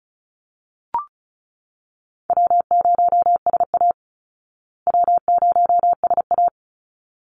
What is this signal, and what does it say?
Morse code, heard as a keyed tone, about 34 wpm, 700 Hz. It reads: W0SA W0SA
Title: Easy Call Signs Repeated 2x - 35wpm